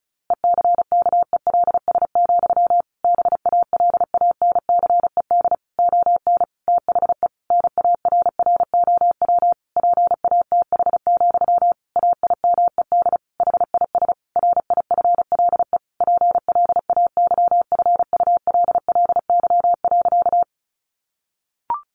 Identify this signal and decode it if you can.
Morse code, heard as a keyed tone, about 34 wpm, 700 Hz. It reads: ECKELS, BALANCED ON THE NARROW PATH, AIMED HIS RIFLE PLAYFULLY.
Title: A Sound Of Thunder 35wpm